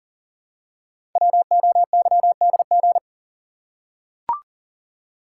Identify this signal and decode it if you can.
Morse code, heard as a keyed tone, about 39 wpm, 700 Hz. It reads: WOYDG
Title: Code Groups - 5 Letters - 40wpm